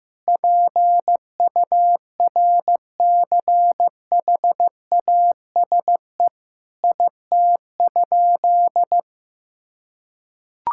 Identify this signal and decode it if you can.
Morse code, heard as a keyed tone, about 15 wpm, 700 Hz. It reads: PURCHASE IT?
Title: Sentences from Top 1000 Words - Review Part 5 - 15wpm